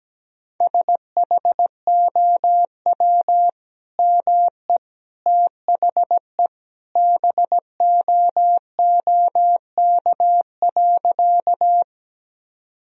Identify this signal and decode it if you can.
Morse code, heard as a keyed tone, about 17 wpm, 700 Hz. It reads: SHOW ME THE BOOK.
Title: Sentences from Top 1000 Words - Review Part 1 - 17wpm